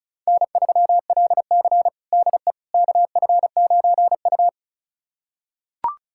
Morse code, heard as a keyed tone, about 34 wpm, 700 Hz. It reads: N3LC DE KF9U
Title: USA Callsign DE Callsign - 4 Characters - 35wpm